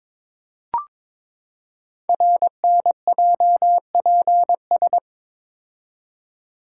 Morse code, heard as a keyed tone, about 22 wpm, 700 Hz. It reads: RNJPS
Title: Code Groups - 5 Letters - 22wpm